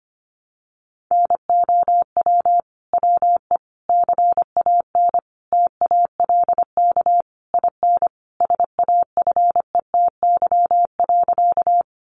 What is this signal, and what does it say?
Morse code, heard as a keyed tone, about 25 wpm, 700 Hz. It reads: NOW WE CAN TALK IN SAFETY.
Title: Sentences from Top 700 Words 25wpm